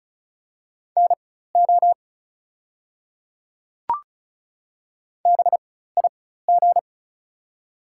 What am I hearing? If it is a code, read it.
Morse code, heard as a keyed tone, about 36 wpm, spaced as for 16 wpm, 700 Hz. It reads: NO BIG
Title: 4x Character Spacing - Top 200 Words - 35wpm